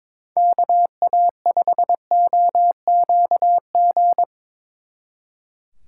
Morse code, heard as a keyed tone, about 24 wpm, 700 Hz. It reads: KA5OQG
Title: Speed Racing - USA Call Signs 22wpm